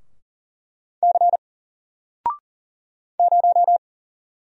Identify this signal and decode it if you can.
Morse code, heard as a keyed tone, about 40 wpm, 700 Hz. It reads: C 0
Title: Single Character (Letter:Number:Punctuation:Prosign) 40wpm